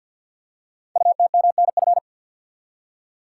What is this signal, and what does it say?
Morse code, heard as a keyed tone, about 53 wpm, 700 Hz. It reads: UTMNF
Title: Code Groups - 5 Letters - 50wpm